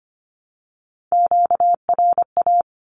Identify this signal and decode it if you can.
Morse code, heard as a keyed tone, about 25 wpm, 700 Hz. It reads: QRA
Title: Q-Codes 25wpm